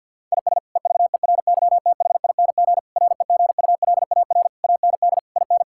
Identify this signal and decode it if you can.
Morse code, heard as a keyed tone, about 50 wpm, 700 Hz. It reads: IS EVERYTHING REGULAR AND IN
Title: Sentences from Top 1000 Words - Review Part 6 - 50wpm